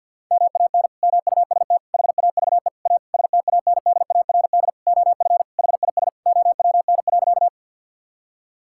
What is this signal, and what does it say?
Morse code, heard as a keyed tone, about 50 wpm, 700 Hz. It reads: MAN MUST HAVE A STANDARD OF HIS OWN.
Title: Sentences from Top 1000 Words - Review Part 6 - 50wpm